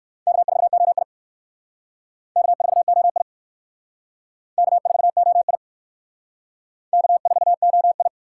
Morse code, heard as a keyed tone, about 60 wpm, 700 Hz. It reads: K4OI K4OI K4OI K4OI